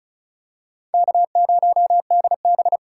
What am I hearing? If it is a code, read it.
Morse code, heard as a keyed tone, about 36 wpm, 700 Hz. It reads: K0DB
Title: USA Call Signs - 2x Character Spacing - 35wpm